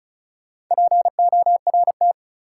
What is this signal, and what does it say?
Morse code, heard as a keyed tone, about 36 wpm, 700 Hz. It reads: PORT
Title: Sentences from Top 1000 Words - Review Part 1 - New Word Drill - 35wpm